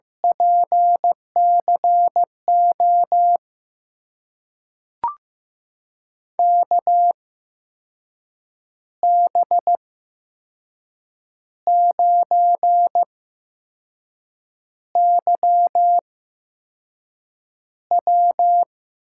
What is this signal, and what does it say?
Morse code, heard as a keyed tone, about 15 wpm, spaced as for 5 wpm, 700 Hz. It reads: PCO KB9YW
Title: USA Call Signs - 6 Characters - 8x Character Spacing - 15wpm